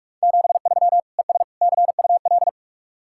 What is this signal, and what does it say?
Morse code, heard as a keyed tone, about 48 wpm, 700 Hz. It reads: 73 ES CUL